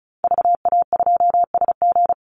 Morse code, heard as a keyed tone, about 34 wpm, 700 Hz. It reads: VA2SG
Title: Call Signs Repeated 2x - 35wpm